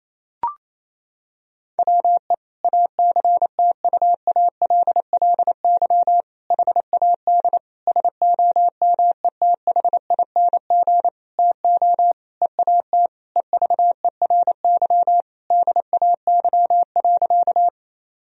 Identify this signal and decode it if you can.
Morse code, heard as a keyed tone, about 28 wpm, 700 Hz. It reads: WE ACTUALLY HAD SOMETHING TO EAT EVERY DAY.
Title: Sentences from Top 1000 Words - Review Part 4 - 28wpm